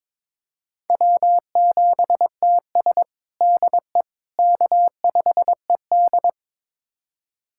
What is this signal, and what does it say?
Morse code, heard as a keyed tone, about 22 wpm, 700 Hz. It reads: W7TS DE K5ED
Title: USA Callsign DE Callsign - 4 Characters - 22wpm